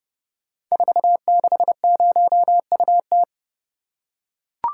Morse code, heard as a keyed tone, about 30 wpm, 700 Hz. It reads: 460UT
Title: Code Groups - 5 Letter:Numbers - 30wpm